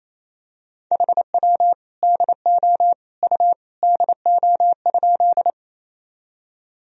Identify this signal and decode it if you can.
Morse code, heard as a keyed tone, about 28 wpm, 700 Hz. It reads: HW DO U DO?
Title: Sentences from Top 100 Words and Abbreviations 28wpm